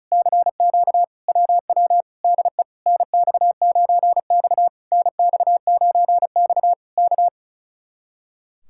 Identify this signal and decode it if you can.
Morse code, heard as a keyed tone, about 36 wpm, 700 Hz. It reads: CQ WW DE NX9X NX9X K
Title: Speed-Racing Common QSO Phrases - 35wpm